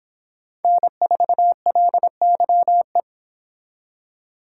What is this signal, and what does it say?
Morse code, heard as a keyed tone, about 29 wpm, 700 Hz. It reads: N4LYE